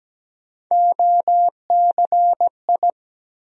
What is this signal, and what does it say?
Morse code, heard as a keyed tone, about 17 wpm, 700 Hz. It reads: OCI
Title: USA Call Sign Postfixes - 3 Letters 17wpm